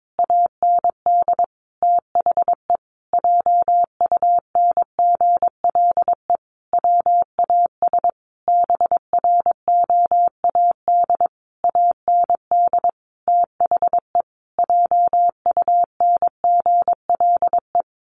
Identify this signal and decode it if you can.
Morse code, heard as a keyed tone, about 22 wpm, 700 Hz. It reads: AND THE JUNGLE WAS BROAD AND THE JUNGLE
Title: A Sound Of Thunder 22wpm